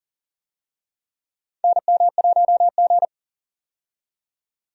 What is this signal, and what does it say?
Morse code, heard as a keyed tone, about 40 wpm, 700 Hz. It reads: NM1G
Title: USA Call Signs - 4 Characters - 40wpm